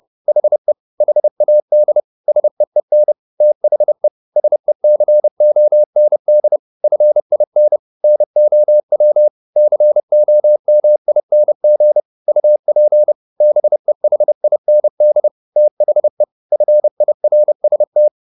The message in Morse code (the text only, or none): HE HAD SEEN THE SECOND FIN NOW COMING UP BEHIND THE FIRST